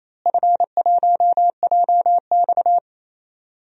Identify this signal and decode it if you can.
Morse code, heard as a keyed tone, about 28 wpm, 700 Hz. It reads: F1JX
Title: DXCC-100 Call Signs - 4 Characters - Repeated 3x - 28wpm